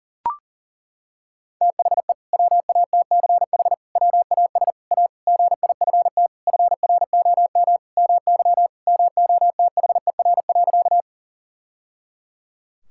Morse code, heard as a keyed tone, about 40 wpm, 700 Hz. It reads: THE WATCH WAS A GIFT FROM MY MOTHER.
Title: Sentences from Top 1000 Words - Review Part 6 - 40wpm